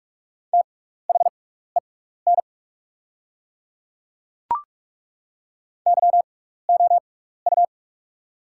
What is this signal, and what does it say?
Morse code, heard as a keyed tone, about 44 wpm, spaced as for 15 wpm, 700 Hz. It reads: THEN YOU
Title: 6x Character Spacing - Top 100 Words - 45wpm